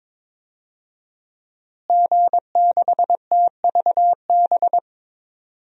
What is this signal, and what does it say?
Morse code, heard as a keyed tone, about 22 wpm, 700 Hz. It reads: G6TVB
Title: DXCC-100 Call Signs - 5 Characters - Repeated 3x - 22wpm